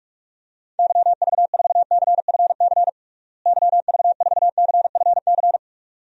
Morse code, heard as a keyed tone, about 46 wpm, 700 Hz. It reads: YV4CFC YV4CFC
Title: DXCC-100 Call Signs - 6 Characters - Repeated 2x - 45wpm